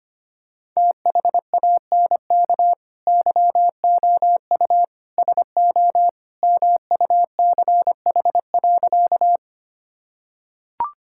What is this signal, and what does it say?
Morse code, heard as a keyed tone, about 25 wpm, 700 Hz. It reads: THANK YOU SO MUCH.
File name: practice / 25wpm / Sentences with 2-Word Phrases - 25wpm